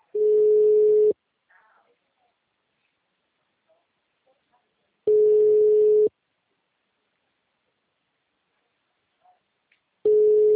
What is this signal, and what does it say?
Ringback tone (Europe, 425 Hz).